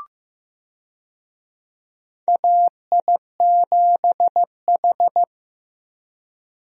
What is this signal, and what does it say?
Morse code, heard as a keyed tone, about 15 wpm, 700 Hz. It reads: AI7H
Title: USA Call Signs - 4 Characters - Repeated 2x - 15wpm